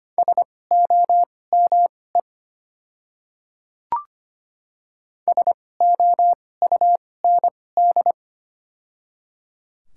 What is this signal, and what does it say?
Morse code, heard as a keyed tone, about 25 wpm, spaced as for 18 wpm, 700 Hz. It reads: SOME SOUND
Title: 2x Character Spacing - Top 100 Words - 25wpm